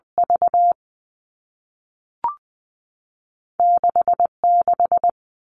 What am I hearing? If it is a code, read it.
Morse code, heard as a keyed tone, about 20 wpm, 700 Hz. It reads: V 6 6
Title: All Cut Numbers - 2-Digit 20wpm